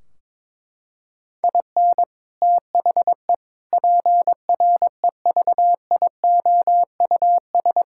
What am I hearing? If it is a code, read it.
Morse code, heard as a keyed tone, about 22 wpm, 700 Hz. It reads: IN THE PREVIOUS
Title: Sentences from Top 1600 Words 22wpm